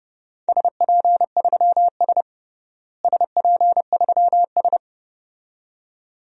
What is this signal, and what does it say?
Morse code, heard as a keyed tone, about 30 wpm, 700 Hz. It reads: SP3S SP3S
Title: DXCC-100 Call Signs - 4 Characters - Repeated 2x - 30wpm